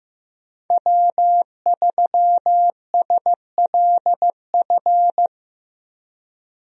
Morse code, heard as a keyed tone, about 15 wpm, 700 Hz. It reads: W3SLF
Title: USA Call Signs - 5 Characters - Repeated 3x - 15wpm